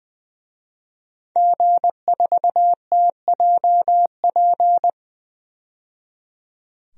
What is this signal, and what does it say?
Morse code, heard as a keyed tone, about 20 wpm, 700 Hz. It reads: G4TJP
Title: DXCC-100 Call Signs - 5 Characters - 20wpm